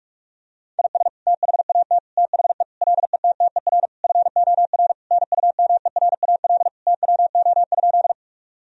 Morse code, heard as a keyed tone, about 43 wpm, 700 Hz. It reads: IS THAT THE LETTER FOR NUMERAL TWO?
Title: Sentences from Top 1000 Words - Review Part 2 - 45wpm